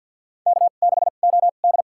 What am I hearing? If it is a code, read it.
Morse code, heard as a keyed tone, about 52 wpm, spaced as for 37 wpm, 700 Hz. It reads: K6OD
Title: USA Call Signs - 4 Characters - 2x Character Spacing - 50wpm